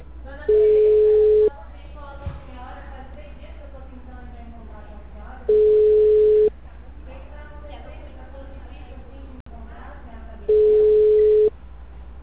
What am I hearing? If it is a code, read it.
Ringback tone (Europe, 425 Hz), with speech.